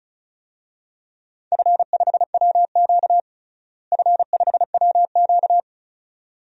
Morse code, heard as a keyed tone, about 34 wpm, 700 Hz. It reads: F5WQ F5WQ
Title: DXCC-100 Call Signs - 4 Characters - Repeated 2x - 35wpm